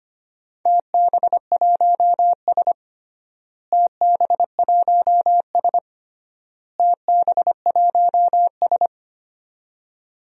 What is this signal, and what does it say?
Morse code, heard as a keyed tone, about 25 wpm, 700 Hz. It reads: TB1S TB1S TB1S
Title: DXCC-100 Call Signs - 4 Characters - Repeated 3x - 25wpm